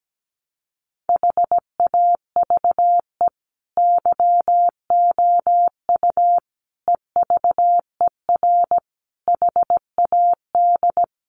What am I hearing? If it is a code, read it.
Morse code, heard as a keyed tone, about 17 wpm, 700 Hz. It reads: HAVE YOU EVER HAD
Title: Sentences from Top 1200 Words 17wpm